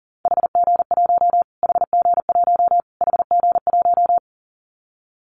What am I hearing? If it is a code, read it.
Morse code, heard as a keyed tone, about 40 wpm, 700 Hz. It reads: HG1 HG1 HG1
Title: Callsign Prefixes for Top 25 DXCC - Repeated 3x - 40wpm